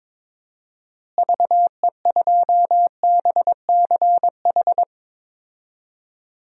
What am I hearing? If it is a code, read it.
Morse code, heard as a keyed tone, about 22 wpm, 700 Hz. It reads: VE2BCH
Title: DXCC-100 Call Signs - 6 Characters - Repeated 2x - 22wpm